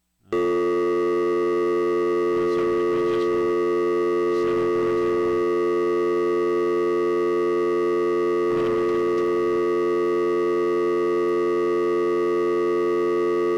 Dial tone (North America), 350 + 440 Hz.